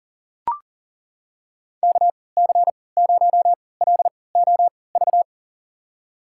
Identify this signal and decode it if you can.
Morse code, heard as a keyed tone, about 37 wpm, spaced as for 22 wpm, 700 Hz. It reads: KC0LOV
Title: USA Call Signs - 3x Character Spacing - 40wpm